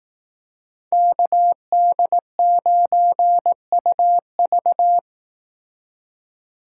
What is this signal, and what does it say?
Morse code, heard as a keyed tone, about 18 wpm, 700 Hz. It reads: KD9UV
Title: Speed Racing - USA 2x2 Call Signs 15wpm